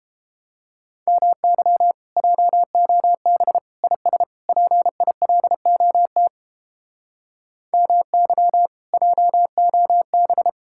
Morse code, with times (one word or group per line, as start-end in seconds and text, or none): 1.1-1.9 MY
2.2-3.6 JOB
3.8-4.2 IS
4.5-6.3 PILOT
7.7-8.7 MY
8.9-10.5 JOB